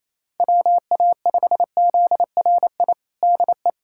Morse code, heard as a keyed tone, about 28 wpm, 700 Hz. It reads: WA5ZRI DE
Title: USA Callsign DE Callsign 28wpm